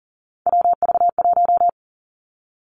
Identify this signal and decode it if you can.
Morse code, heard as a keyed tone, about 40 wpm, 700 Hz. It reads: WV1
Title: Three Challenging Letter-Number Combinations 40wpm